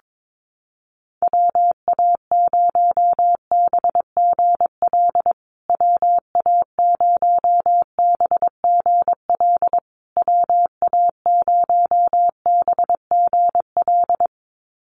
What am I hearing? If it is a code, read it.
Morse code, heard as a keyed tone, about 22 wpm, 700 Hz. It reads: WA0BGL WA0BGL WA0BGL
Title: USA Call Signs Repeated 3x 22wpm